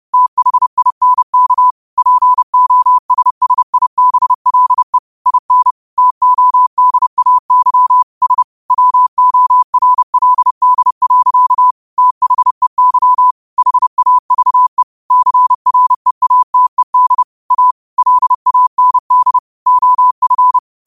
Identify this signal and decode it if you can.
Morse code, heard as a keyed tone, about 30 wpm, 1000 Hz. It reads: THINK POSSIBLE IN TODAY S WORLD. THEY HAVE CREATED A LAND OF
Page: Quotes for Fri, 15 Aug 2025 in Morse Code at 30 words per minute.